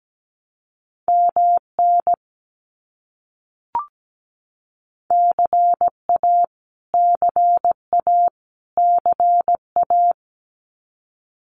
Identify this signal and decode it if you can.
Morse code, heard as a keyed tone, about 17 wpm, 700 Hz. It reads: MN CA CA CA
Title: US States Repeated 3x 17wpm